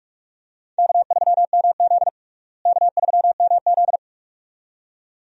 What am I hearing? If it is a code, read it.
Morse code, heard as a keyed tone, about 46 wpm, 700 Hz. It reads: K3MZ K3MZ
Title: USA Call Signs - 4 Characters - Repeated 2x - 45wpm